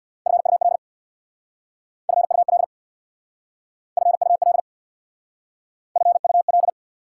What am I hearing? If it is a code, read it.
Morse code, heard as a keyed tone, about 66 wpm, 700 Hz. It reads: FUL FUL FUL FUL